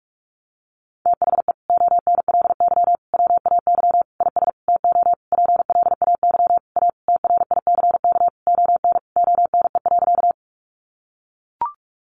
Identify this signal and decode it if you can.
Morse code, heard as a keyed tone, about 44 wpm, 700 Hz. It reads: THE ONLY WAY IS TO PLAY A TRICK ON ONE.
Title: Sentences from Top 1500 Words 45wpm